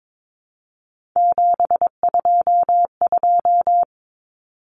Morse code, heard as a keyed tone, about 22 wpm, 700 Hz. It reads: 722
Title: Common Cut Numbers - 3-Digit 22wpm